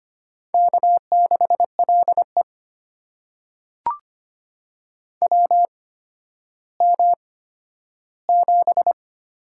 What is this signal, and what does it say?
Morse code, heard as a keyed tone, about 25 wpm, spaced as for 12 wpm, 700 Hz. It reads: K6LE WM7
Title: USA Call Signs - 4 Characters - 8x Character Spacing - 25wpm